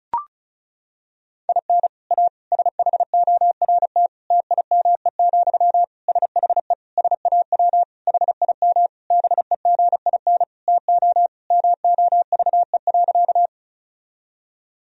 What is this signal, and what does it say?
Morse code, heard as a keyed tone, about 36 wpm, 700 Hz. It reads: IN A SHORT TIME, SHE SAW HIM BEGIN TO MOVE.
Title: Sentences from Top 1000 Words - Review Part 2 - 35wpm